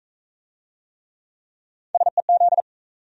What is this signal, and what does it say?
Morse code, heard as a keyed tone, about 39 wpm, 700 Hz. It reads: SEZ